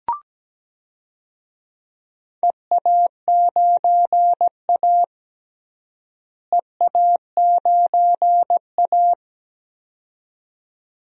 Morse code, heard as a keyed tone, about 17 wpm, 700 Hz. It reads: EA9A EA9A
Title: DXCC-100 Call Signs - 4 Characters - Repeated 2x - 17wpm